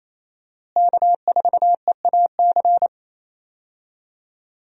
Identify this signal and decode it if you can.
Morse code, heard as a keyed tone, about 28 wpm, 700 Hz. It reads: K4EAC
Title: USA Call Signs - 5 Characters - Repeated 3x - 28wpm